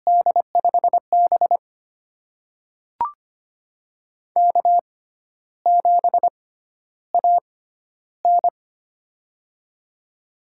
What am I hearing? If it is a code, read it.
Morse code, heard as a keyed tone, about 25 wpm, spaced as for 12 wpm, 700 Hz. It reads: D5B K7AN
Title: USA Call Signs - 4 Characters - 6x Character Spacing - 25wpm